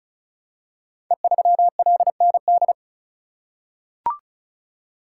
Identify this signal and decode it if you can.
Morse code, heard as a keyed tone, about 36 wpm, 700 Hz. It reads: E3LND
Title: Code Groups - 5 Letter:Numbers - 35wpm